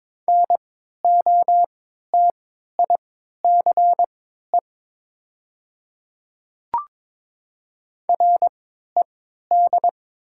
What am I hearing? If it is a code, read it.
Morse code, heard as a keyed tone, about 22 wpm, spaced as for 13 wpm, 700 Hz. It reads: NOTICE RED
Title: 3x Character Spacing - Top 500 Words - 22wpm